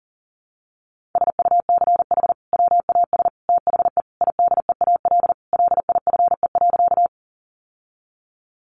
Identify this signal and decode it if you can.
Morse code, heard as a keyed tone, about 40 wpm, 700 Hz. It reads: SUCH WAS THE IDEAL LIFE.
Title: Sentences from Top 1400 Words 40wpm